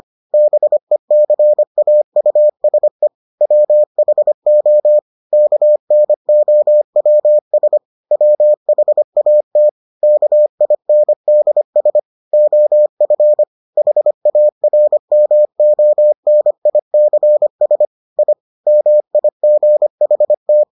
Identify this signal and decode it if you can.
Morse code, heard as a keyed tone, about 25 wpm, 600 Hz. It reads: BECAUSE WHO KNOWS WHAT KINDS OF HARMONICS I MIGHT